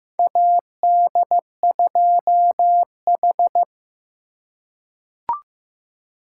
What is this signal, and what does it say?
Morse code, heard as a keyed tone, about 15 wpm, 700 Hz. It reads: AD2H
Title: USA Call Signs - 4 Characters - 3x Character Spacing - 15wpm